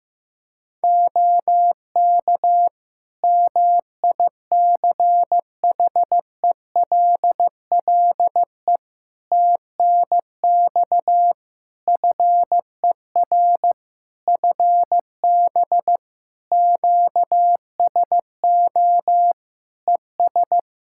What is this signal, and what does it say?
Morse code, heard as a keyed tone, about 15 wpm, 700 Hz. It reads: OK MICHELLE TNX FER FB QSO ES
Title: Practice QSO - Exchange 4 - Ending - 15wpm